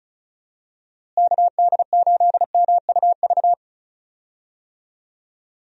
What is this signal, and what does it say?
Morse code, heard as a keyed tone, about 34 wpm, 700 Hz. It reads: KD8MUV
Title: USA Call Signs - 6 Characters - Repeated 3x - 35wpm